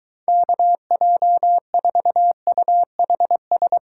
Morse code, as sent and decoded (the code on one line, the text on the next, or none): -.- .--- ....- ..- .... ...
KJ4UHS